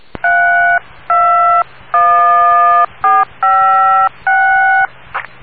DTMF, heard as telephone touch tones, tones of about 540 ms. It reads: 321456